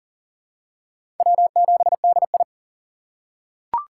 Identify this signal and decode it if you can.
Morse code, heard as a keyed tone, about 40 wpm, 700 Hz. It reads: W7DI
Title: USA Call Signs - 4 Characters - 4x Character Spacing - 40wpm